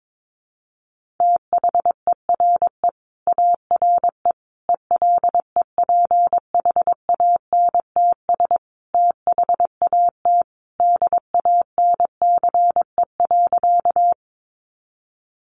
Morse code, heard as a keyed tone, about 22 wpm, 700 Hz. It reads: THERE ARE ELEPHANTS THAT DANCE.
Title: Easy Phrases 22wpm